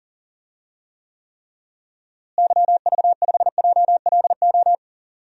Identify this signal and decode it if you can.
Morse code, heard as a keyed tone, about 38 wpm, 700 Hz. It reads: YV5JLO
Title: DXCC-100 Call Signs - 6 Characters - Repeated 3x - 40wpm